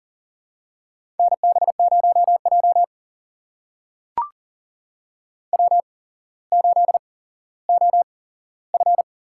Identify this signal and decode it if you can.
Morse code, heard as a keyed tone, about 40 wpm, spaced as for 17 wpm, 700 Hz. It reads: NB0J W8OF
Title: USA Call Signs - 4 Characters - 8x Character Spacing - 40wpm